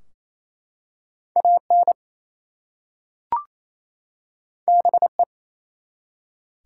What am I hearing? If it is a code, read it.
Morse code, heard as a keyed tone, about 27 wpm, 700 Hz. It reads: AN BE
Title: Two Letter Words - 28wpm